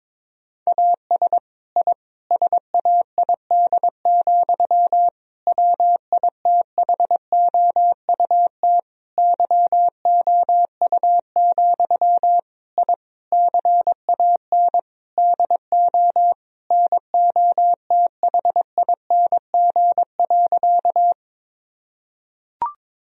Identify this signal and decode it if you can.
Morse code, heard as a keyed tone, about 22 wpm, 700 Hz. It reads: AS I SAID, WITHOUT YOU, I CAN DO NOTHING.
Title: Sentences from Top 1000 Words - Review Part 3 - 22wpm